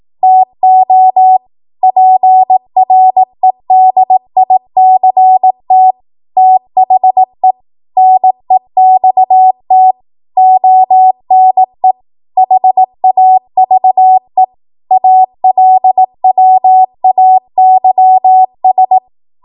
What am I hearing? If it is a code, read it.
Morse code, heard as a keyed tone, about 18 wpm, 750 Hz. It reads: TO PREDICT THE NEXT ONE HAVE ALWAYS
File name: Morse / 18WPM